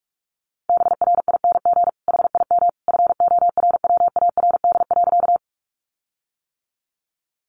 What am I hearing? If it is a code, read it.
Morse code, heard as a keyed tone, about 44 wpm, 700 Hz. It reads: BRING HIM FORWARD.
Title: Sentences from Top 1300 Words 45wpm